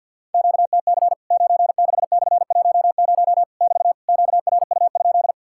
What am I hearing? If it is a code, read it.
Morse code, heard as a keyed tone, about 50 wpm, 700 Hz. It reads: QTC 96/10 <BT> QRU?
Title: Worked All Europe DX Contest - QTC Exchange - 50wpm